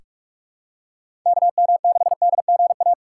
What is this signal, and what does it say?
Morse code, heard as a keyed tone, about 45 wpm, 700 Hz. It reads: KM6DGA
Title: USA Call Signs - 6 Characters - 3x Character Spacing - 45wpm